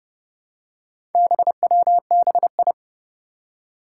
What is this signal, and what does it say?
Morse code, heard as a keyed tone, about 30 wpm, 700 Hz. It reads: BWBI
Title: Four Challenging Letter-Number Combinations 30wpm